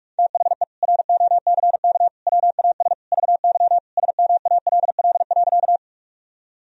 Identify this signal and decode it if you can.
Morse code, heard as a keyed tone, about 46 wpm, 700 Hz. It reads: THE ROCK WAS VY SMALL.
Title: Sentences from Top 400 Words and Abbreviations 45wpm